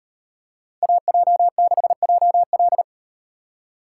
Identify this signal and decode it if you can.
Morse code, heard as a keyed tone, about 35 wpm, 700 Hz. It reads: AJ6JL